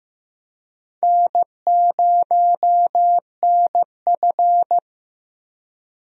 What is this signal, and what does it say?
Morse code, heard as a keyed tone, about 16 wpm, 700 Hz. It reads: N0NF
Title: USA Call Signs - 4 Characters - Repeated 3x - 15wpm